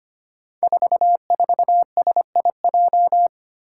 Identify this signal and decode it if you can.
Morse code, heard as a keyed tone, about 25 wpm, 700 Hz. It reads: 44SIJ
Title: Five Challenging Letter-Number Combinations 25wpm